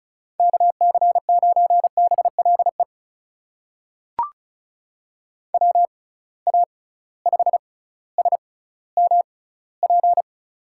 Morse code, heard as a keyed tone, about 36 wpm, spaced as for 18 wpm, 690 Hz. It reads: KC9BLE WA5SMP
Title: USA Call Signs - 6 Characters - 6x Character Spacing - 35wpm